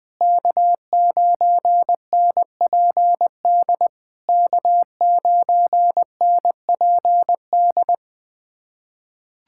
Morse code, heard as a keyed tone, about 20 wpm, 700 Hz. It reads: K9NPD K9NPD
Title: SOTA QSO Practice 20wpm